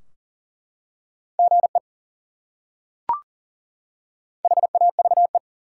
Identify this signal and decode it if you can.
Morse code, heard as a keyed tone, about 39 wpm, 700 Hz. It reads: GE HAVE
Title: Top 180 QSO Elements 40wpm